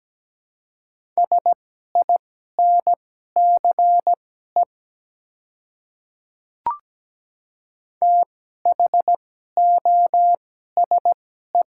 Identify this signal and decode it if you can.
Morse code, heard as a keyed tone, about 17 wpm, spaced as for 12 wpm, 690 Hz. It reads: SINCE THOSE
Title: 2x Character Spacing - Top 300 Words - 17wpm